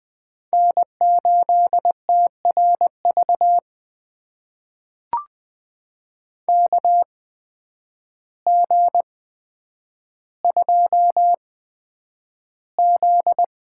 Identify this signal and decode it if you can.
Morse code, heard as a keyed tone, about 20 wpm, spaced as for 9 wpm, 700 Hz. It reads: N8TRV KG2Z
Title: USA Call Signs - 8x Character Spacing - 20wpm